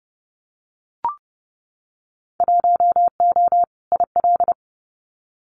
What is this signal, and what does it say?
Morse code, heard as a keyed tone, about 30 wpm, 700 Hz. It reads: 1O IL
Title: Winter Field Day Exchanges 30wpm